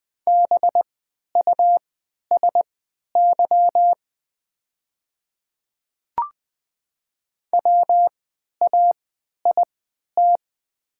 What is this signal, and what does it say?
Morse code, heard as a keyed tone, about 20 wpm, spaced as for 11 wpm, 690 Hz. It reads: BUSY WAIT
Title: 3x Character Spacing - Top 500 Words - 20wpm